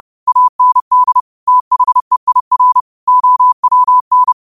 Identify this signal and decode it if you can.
Morse code, heard as a keyed tone, about 30 wpm, 1000 Hz. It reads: AND THEIR OWN
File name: qotdfiles / 30wpm